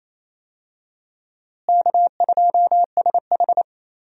Morse code, heard as a keyed tone, about 28 wpm, 700 Hz. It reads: K2SH
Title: USA Call Signs - 4 Characters - Repeated 3x - 28wpm